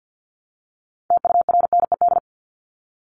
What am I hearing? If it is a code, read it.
Morse code, heard as a keyed tone, about 50 wpm, 700 Hz. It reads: TURNED
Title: ED Suffix - Top 100 Words with Base Words 50wpm